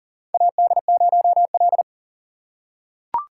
Morse code, heard as a keyed tone, about 40 wpm, 700 Hz. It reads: AD0L
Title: USA Call Signs - 4 Characters - 6x Character Spacing - 40wpm